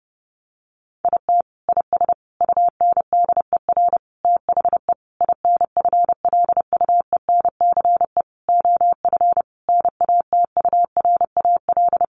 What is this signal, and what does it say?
Morse code, heard as a keyed tone, about 30 wpm, 700 Hz. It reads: IT IS UNDER THE INFLUENCE OF NATURAL
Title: Sentences from Top 1400 Words 30wpm